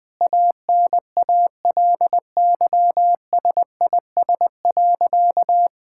Morse code, heard as a keyed tone, about 20 wpm, 700 Hz. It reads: ANALYSIS.
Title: Sentences from Top 1000 Words - Review Part 5 - 20wpm